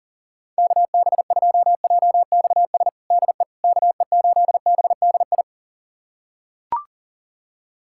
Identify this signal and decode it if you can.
Morse code, heard as a keyed tone, about 40 wpm, 700 Hz. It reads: KB2JXS DE KE8BDI
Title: USA Callsign DE Callsign 40wpm